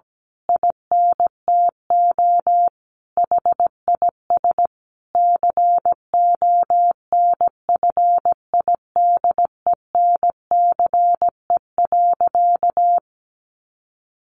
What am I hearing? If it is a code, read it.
Morse code, heard as a keyed tone, about 17 wpm, 700 Hz. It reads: INTO HIS CONFIDENCE.
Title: Sentences from Top 1500 Words 17wpm